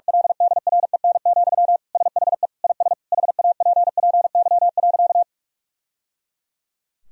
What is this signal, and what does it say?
Morse code, heard as a keyed tone, about 46 wpm, 700 Hz. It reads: LDREN, SHE IS HAPPY.
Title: Sentences from Top 1000 Words - Review Part 3 - 45wpm